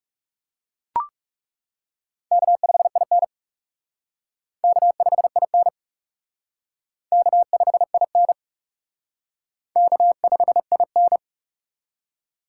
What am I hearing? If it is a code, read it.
Morse code, heard as a keyed tone, about 36 wpm, 700 Hz. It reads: K5IN K5IN K5IN K5IN
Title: Speed Racing - USA 1x2 Call Signs 30wpm